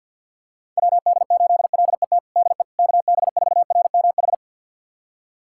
Morse code, heard as a keyed tone, about 50 wpm, 700 Hz. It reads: WD8LET DE KB4RMH
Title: USA Callsign DE Callsign 50wpm